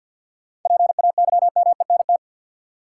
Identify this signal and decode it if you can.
Morse code, heard as a keyed tone, about 53 wpm, 700 Hz. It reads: PAYMENT
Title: Sentences from Top 1200 Words - New Word Drill - 50wpm